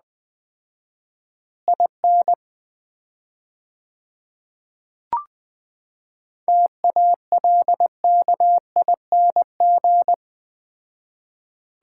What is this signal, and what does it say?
Morse code, heard as a keyed tone, about 20 wpm, 700 Hz. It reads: IN TALKING
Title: 4x Word-Spacing - Sets of 2 Words - 20wpm